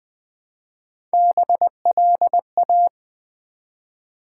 Morse code, heard as a keyed tone, about 20 wpm, 700 Hz. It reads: BLA
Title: Build A Word - 1 Letter At A Time - 3 to 5-Letter Words - 20wpm